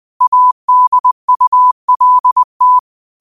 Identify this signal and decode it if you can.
Morse code, heard as a keyed tone, about 20 wpm, 1000 Hz. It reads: ADULT
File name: qotdfiles / 20wpm